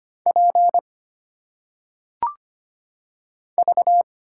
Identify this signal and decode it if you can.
Morse code, heard as a keyed tone, about 25 wpm, 700 Hz. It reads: P V
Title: Single Letters - Phonetic Pronunciation 25wpm